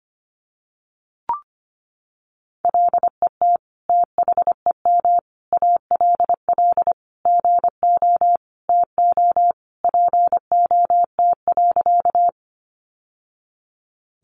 Morse code, heard as a keyed tone, about 25 wpm, 700 Hz. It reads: LET THEM ALL GO TO POT.
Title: Sentences from Top 1000 Words 25wpm